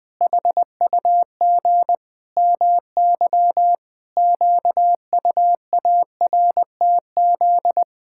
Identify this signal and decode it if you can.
Morse code, heard as a keyed tone, about 20 wpm, 700 Hz. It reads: HUG MY QUARTZ
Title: Pangram Challenge v2 - 20wpm